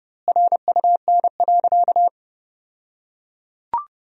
Morse code, heard as a keyed tone, about 30 wpm, 700 Hz. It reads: RUN.
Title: 2x Word-Spacing - Sentences from Top 500 Words - 30wpm